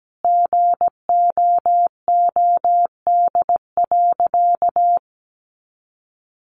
Morse code, heard as a keyed tone, about 17 wpm, 700 Hz. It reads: GOOD.
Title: Sentences from Top 900 Words 17wpm